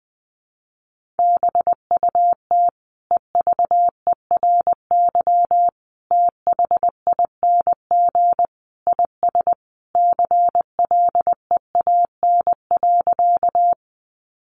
Morse code, heard as a keyed tone, about 20 wpm, 700 Hz. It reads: BUT EVERY THING IS CLEAN.
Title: Sentences from Top 1100 Words 20wpm